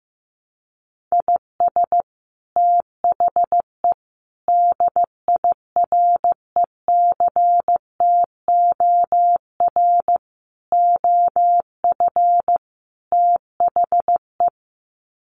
Morse code, heard as a keyed tone, about 15 wpm, 700 Hz. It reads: IS THE DIRECTOR OF THE
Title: Sets of 5 Words Encore2 15wpm